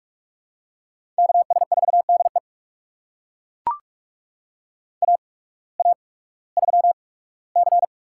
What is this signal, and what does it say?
Morse code, heard as a keyed tone, about 42 wpm, spaced as for 21 wpm, 700 Hz. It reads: KS4DE AA3C
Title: USA Call Signs - 8x Character Spacing - 45wpm